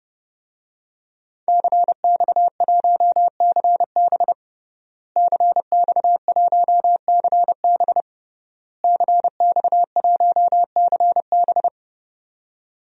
Morse code, heard as a keyed tone, about 30 wpm, 700 Hz. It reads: CX1CB CX1CB CX1CB
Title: DXCC-100 Call Signs - 5 Characters - Repeated 3x - 30wpm